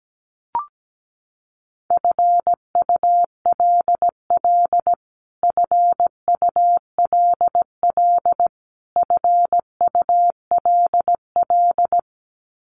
Morse code, heard as a keyed tone, about 17 wpm, 700 Hz. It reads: FULL FULL FULL
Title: Top 500 Words Repeated 17wpm